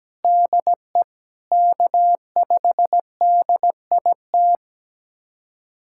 Morse code, heard as a keyed tone, about 17 wpm, 700 Hz. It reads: DE K5DIT
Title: USA Callsign DE Callsign - 5 Characters - 17wpm